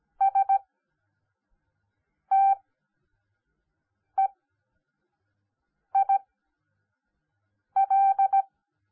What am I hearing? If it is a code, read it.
Morse code, heard as a keyed tone, about 17 wpm, spaced as for 5 wpm, 780 Hz. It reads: STEIL
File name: Onlinelehrgang / morsen / DJ4UF Morsen10